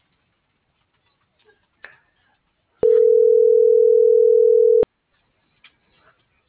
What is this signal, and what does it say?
Ringback tone (North America), 440 + 480 Hz.